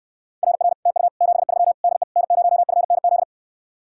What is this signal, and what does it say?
Morse code, heard as a keyed tone, about 69 wpm, 700 Hz. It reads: RR TU 73 DE N0FNZ